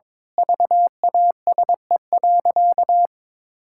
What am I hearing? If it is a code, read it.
Morse code, heard as a keyed tone, about 22 wpm, 700 Hz. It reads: VASE.
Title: Pangram Challenge v2 - 22wpm